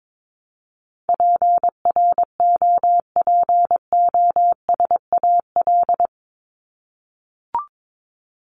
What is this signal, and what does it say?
Morse code, heard as a keyed tone, about 22 wpm, 700 Hz. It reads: PROPOSAL
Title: Top 1751-2000 Words 22wpm